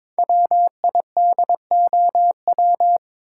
Morse code, heard as a keyed tone, about 22 wpm, 700 Hz. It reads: WIDOW
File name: 5 Letter Words - First 3 Letters Given - 22wpm